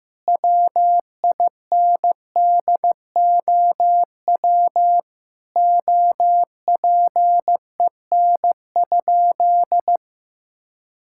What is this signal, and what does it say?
Morse code, heard as a keyed tone, about 15 wpm, 700 Hz. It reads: WINDOW OPEN?
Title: Sentences from Top 1600 Words 15wpm